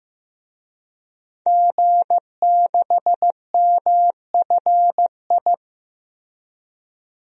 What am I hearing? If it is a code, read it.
Morse code, heard as a keyed tone, about 15 wpm, 700 Hz. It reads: G6MFI
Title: DXCC-100 Call Signs - 5 Characters - 15wpm